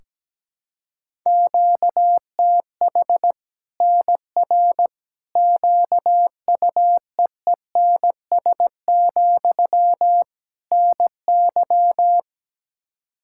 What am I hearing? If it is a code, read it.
Morse code, heard as a keyed tone, about 17 wpm, 700 Hz. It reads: QTH NR QUEENS, NY
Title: Practice QSO - Exchange 1 - Beginning - 17wpm